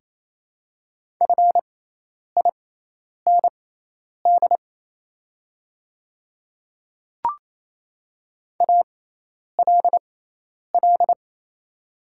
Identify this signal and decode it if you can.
Morse code, heard as a keyed tone, about 27 wpm, spaced as for 10 wpm, 690 Hz. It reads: FIND ALL
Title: 6x Character Spacing - Top 100 Words - 28wpm